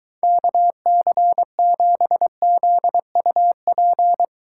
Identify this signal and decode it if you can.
Morse code, heard as a keyed tone, about 23 wpm, 700 Hz. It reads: KC7ZUP